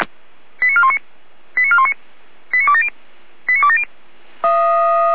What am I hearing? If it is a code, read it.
Five-tone selective call (ZVEI), heard as quick runs of five tones, tones of about 80 ms. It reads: 89419 89419 89270 89270